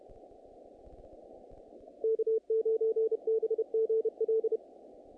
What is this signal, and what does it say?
Morse code, heard as a keyed tone, about 31 wpm, 440 Hz. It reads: K9BGL
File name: contests / naqp-cw-k5tr-op-k5pi- -2100